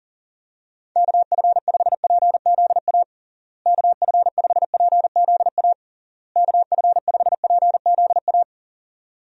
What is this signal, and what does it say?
Morse code, heard as a keyed tone, about 40 wpm, 700 Hz. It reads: KF5PZA KF5PZA KF5PZA
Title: USA Call Signs - 6 Characters - Repeated 3x - 40wpm